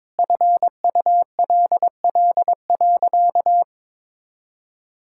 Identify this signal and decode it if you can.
Morse code, heard as a keyed tone, about 22 wpm, 700 Hz. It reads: FULL.
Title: Sentences from Top 1600 Words 22wpm